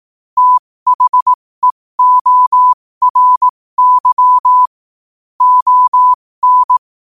Morse code, heard as a keyed tone, about 18 wpm, spaced as for 15 wpm, 1000 Hz. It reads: THEORY ON
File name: qotdfiles / 15wpm